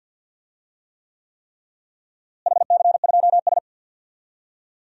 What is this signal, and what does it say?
Morse code, heard as a keyed tone, about 48 wpm, 700 Hz. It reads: HC2S
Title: DXCC-100 Call Signs - 4 Characters - Repeated 3x - 50wpm